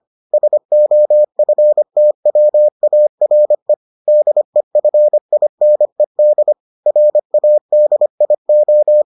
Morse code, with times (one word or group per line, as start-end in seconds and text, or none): 0.3-3.7 SOFTWARE
4.1-6.5 DEFINED
6.9-9.0 RADIO